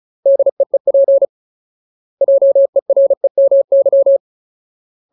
Morse code, heard as a keyed tone, about 37 wpm, 550 Hz. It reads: DEEP JEREMY
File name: Ultimate Word List / 35WPM Ultimate